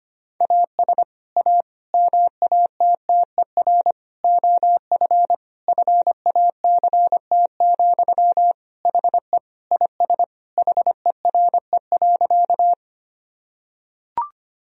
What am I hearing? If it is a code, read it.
Morse code, heard as a keyed tone, about 25 wpm, 700 Hz. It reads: AS A MATTER OF FACT, HE IS HERE.
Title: Sentences from Top 1000 Words - Review Part 3 - 25wpm